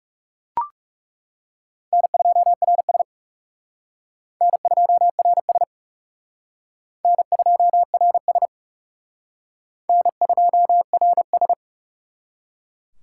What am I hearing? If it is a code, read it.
Morse code, heard as a keyed tone, about 36 wpm, 700 Hz. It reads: N2RS N2RS N2RS N2RS
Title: Speed Racing - USA 1x2 Call Signs 30wpm